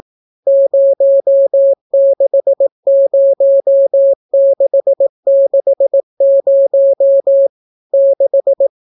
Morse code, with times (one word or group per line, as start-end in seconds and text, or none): 0.5-7.5 060660
7.9-8.7 6